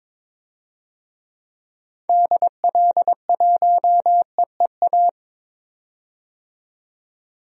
Morse code, heard as a keyed tone, about 22 wpm, 700 Hz. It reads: DL1EEA
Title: DXCC-100 Call Signs - 6 Characters - 22wpm